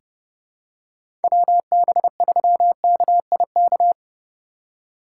Morse code, heard as a keyed tone, about 30 wpm, 700 Hz. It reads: WB3KIK
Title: USA Call Signs - 2x Character Spacing - 30wpm